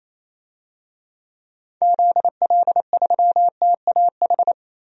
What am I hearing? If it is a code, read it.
Morse code, heard as a keyed tone, about 28 wpm, 700 Hz. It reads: ZL3TAH
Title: DXCC-100 Call Signs - 6 Characters - Repeated 3x - 28wpm